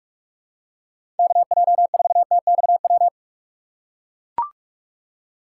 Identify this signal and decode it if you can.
Morse code, heard as a keyed tone, about 43 wpm, 700 Hz. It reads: KJ4TXW
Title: USA Call Signs - 6x Character Spacing - 45wpm